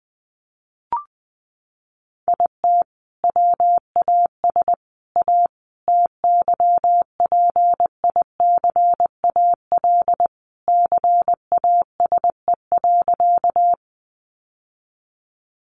Morse code, heard as a keyed tone, about 20 wpm, 700 Hz. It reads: IT WAS A TYPICAL CASE.
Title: Sentences from Top 1100 Words 20wpm